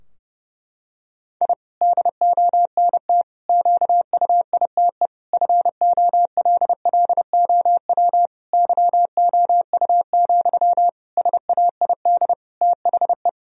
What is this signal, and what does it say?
Morse code, heard as a keyed tone, about 30 wpm, 700 Hz. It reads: I DONT QUITE FOLLOW YOU, SAID THE
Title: Sentences from Top 1000 Words - Review Part 4 - 30wpm